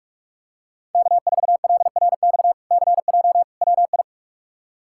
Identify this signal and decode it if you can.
Morse code, heard as a keyed tone, about 46 wpm, 700 Hz. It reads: K4LRX CJ WI
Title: K1USN SST QSO Practice - 45wpm